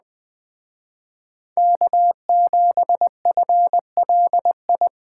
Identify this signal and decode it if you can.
Morse code, heard as a keyed tone, about 20 wpm, 700 Hz. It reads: K7FLI